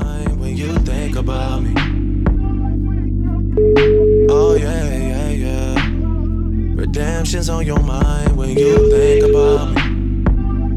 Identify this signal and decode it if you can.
Ringback tone (Europe, 425 Hz), with speech.